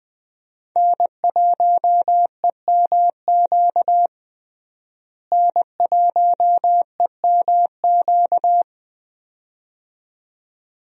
Morse code, heard as a keyed tone, about 20 wpm, 700 Hz. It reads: N1EMQ N1EMQ
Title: USA Call Signs - 5 Characters - Repeated 2x - 20wpm